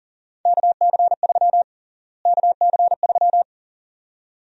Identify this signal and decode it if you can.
Morse code, heard as a keyed tone, about 40 wpm, 700 Hz. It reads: KC3 KC3
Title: USA Call Sign Prefixes - Repeated 2x - 40wpm